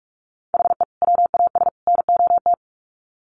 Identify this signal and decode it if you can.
Morse code, heard as a keyed tone, about 44 wpm, 700 Hz. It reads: HE WAS NOT
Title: Sets of 3 Words 45wpm